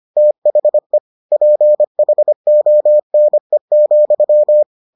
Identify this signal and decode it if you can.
Morse code, heard as a keyed tone, about 25 wpm, 600 Hz. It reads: THE PHONE,